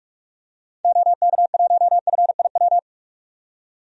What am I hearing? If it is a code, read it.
Morse code, heard as a keyed tone, about 46 wpm, 700 Hz. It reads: OK1FIW
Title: DXCC-100 Call Signs - 6 Characters - 45wpm